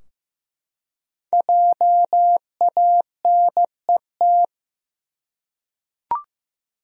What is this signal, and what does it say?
Morse code, heard as a keyed tone, about 15 wpm, 700 Hz. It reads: JANET
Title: USA Common Names - 15wpm